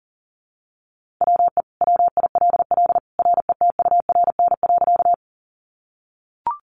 Morse code, heard as a keyed tone, about 40 wpm, 690 Hz. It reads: WE WILL RETURN.
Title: Sentences from Top 800 Words 40wpm